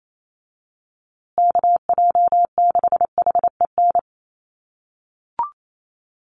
Morse code, heard as a keyed tone, about 28 wpm, 700 Hz. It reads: KJ6HEN
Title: USA Call Signs Repeated 2x 28wpm